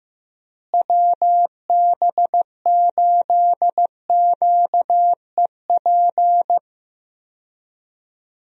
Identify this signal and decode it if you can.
Morse code, heard as a keyed tone, about 15 wpm, 700 Hz. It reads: WB8QEP
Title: USA Call Signs - 6 Characters - Repeated 2x - 15wpm